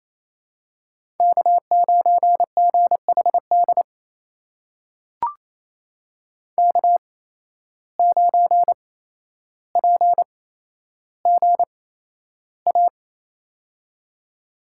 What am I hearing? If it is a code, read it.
Morse code, heard as a keyed tone, about 28 wpm, spaced as for 12 wpm, 700 Hz. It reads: K9GHD K9PGA
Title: USA Call Signs - 5 Characters - 8x Character Spacing - 28wpm